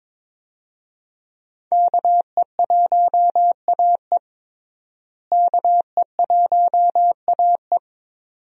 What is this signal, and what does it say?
Morse code, heard as a keyed tone, about 22 wpm, 700 Hz. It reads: KE1AE KE1AE
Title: USA Call Signs - 5 Characters - Repeated 2x - 22wpm